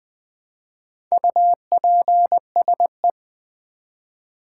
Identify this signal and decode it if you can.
Morse code, heard as a keyed tone, about 20 wpm, 700 Hz. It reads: UPSE
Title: Build A Word - 1 Letter At A Time - 3 to 5-Letter Words - 20wpm